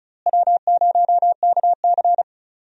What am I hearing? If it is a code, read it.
Morse code, heard as a keyed tone, about 36 wpm, 700 Hz. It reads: W0KC
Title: USA Call Signs - 4 Characters - Repeated 3x - 35wpm